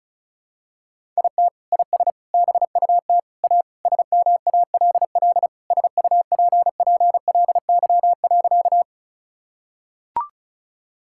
Morse code, heard as a keyed tone, about 34 wpm, 700 Hz. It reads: IT IS BUT A SMALL SUPPLY.
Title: Sentences from Top 1000 Words - Review Part 5 - 35wpm